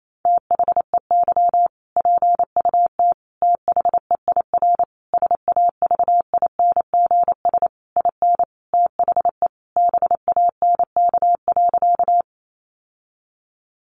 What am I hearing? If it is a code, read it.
Morse code, heard as a keyed tone, about 28 wpm, 700 Hz. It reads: THEY PUT THEIR SAVINGS IN THE BANK.
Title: Sentences from Top 1200 Words 28wpm